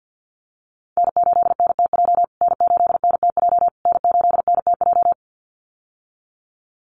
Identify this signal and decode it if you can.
Morse code, heard as a keyed tone, about 48 wpm, 700 Hz. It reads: N8NTJ N8NTJ N8NTJ
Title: USA Call Signs Repeated 3x 50wpm